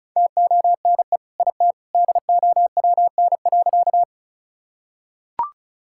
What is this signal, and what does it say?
Morse code, heard as a keyed tone, about 34 wpm, 700 Hz. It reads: TONE IT DOWN.
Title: Sentences from Top 1600 Words 35wpm